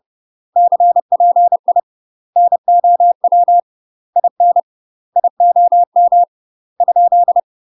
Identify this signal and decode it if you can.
Morse code, heard as a keyed tone, about 30 wpm, 700 Hz. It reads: CPI NOW IN IOM ?
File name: UK MASTER QSOs TRAINING / 30WPM / UK MASTER QSOs 0013